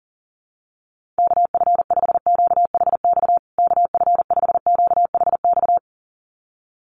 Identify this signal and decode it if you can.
Morse code, heard as a keyed tone, about 40 wpm, 700 Hz. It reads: KF5QHX KF5QHX
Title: USA Call Signs Repeated 2x 40wpm